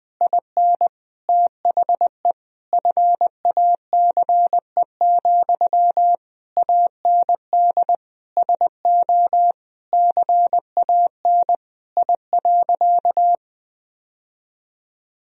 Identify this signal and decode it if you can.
Morse code, heard as a keyed tone, about 20 wpm, 700 Hz. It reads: IN THE FACE, AND SO CAN I.
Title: Sentences from Top 1000 Words - Review Part 2 - 20wpm